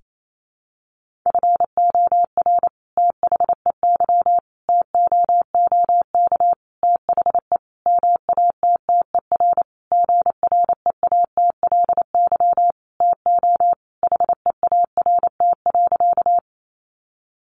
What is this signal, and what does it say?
Morse code, heard as a keyed tone, about 28 wpm, 700 Hz. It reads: FOR THEY TOOK THE MATTER GREATLY TO HEART.
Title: Sentences from Top 1400 Words 28wpm